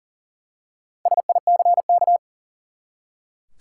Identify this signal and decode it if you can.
Morse code, heard as a keyed tone, about 40 wpm, 700 Hz. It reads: SICK
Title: Four Letter Words - 40wpm